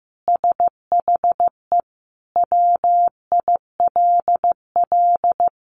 Morse code, heard as a keyed tone, about 14 wpm, 700 Hz. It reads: SHE WILL
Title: Sentences from Top 1500 Words 15wpm